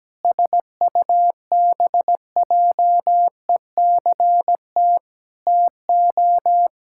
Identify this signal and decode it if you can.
Morse code, heard as a keyed tone, about 17 wpm, 700 Hz. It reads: SUBJECT TO
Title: Pangram Challenge v2 - 17wpm